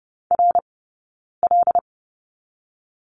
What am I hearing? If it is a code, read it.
Morse code, heard as a keyed tone, about 30 wpm, 700 Hz. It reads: R L
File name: course / 30:10wpm / 086 Lesson 15d - Number 5 - Two Characters 30wpm